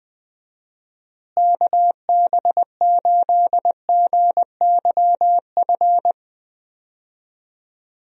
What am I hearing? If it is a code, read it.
Morse code, heard as a keyed tone, about 20 wpm, 700 Hz. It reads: KB8GYF
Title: USA Call Signs - 6 Characters - Repeated 3x - 20wpm